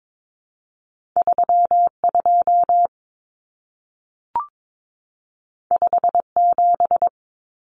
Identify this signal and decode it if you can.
Morse code, heard as a keyed tone, about 22 wpm, 700 Hz. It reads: 32 57
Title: Common Cut Numbers - 2-Digit 22wpm